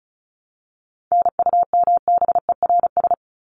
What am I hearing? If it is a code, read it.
Morse code, heard as a keyed tone, about 34 wpm, 700 Hz. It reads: NUMBERS
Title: ES:S Suffix - Top 100 Words with Base Words 35wpm